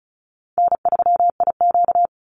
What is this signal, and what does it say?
Morse code, heard as a keyed tone, about 34 wpm, 700 Hz. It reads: N3IQ
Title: Call Signs Repeated 3x - 35wpm